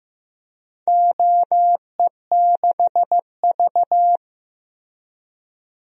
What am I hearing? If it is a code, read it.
Morse code, heard as a keyed tone, about 15 wpm, 700 Hz. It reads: OE6V
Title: DXCC-100 Call Signs - 4 Characters - 15wpm